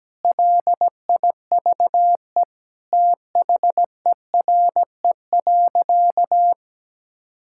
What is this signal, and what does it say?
Morse code, heard as a keyed tone, about 17 wpm, 700 Hz. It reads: LIVE THERE.
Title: Sentences from Top 1000 Words - Review Part 6 - 17wpm